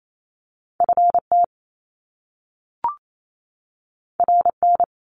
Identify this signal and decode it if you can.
Morse code, heard as a keyed tone, about 28 wpm, 700 Hz. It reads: FT RN
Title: Top Two Letter Combinations in Words 28wpm